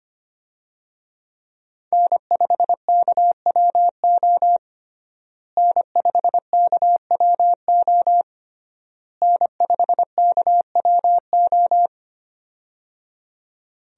Morse code, heard as a keyed tone, about 25 wpm, 700 Hz. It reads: N5KWO N5KWO N5KWO
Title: USA Call Signs - 5 Characters - Repeated 3x - 25wpm